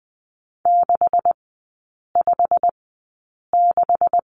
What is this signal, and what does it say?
Morse code, heard as a keyed tone, about 20 wpm, spaced as for 8 wpm, 700 Hz. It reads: 656
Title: 166 Lesson 27e - Number 7 - Three Characters 20wpm